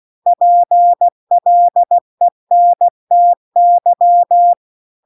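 Morse code, heard as a keyed tone, about 16 wpm, 700 Hz. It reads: PLENTY
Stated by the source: Morse practice text at 16 WPM.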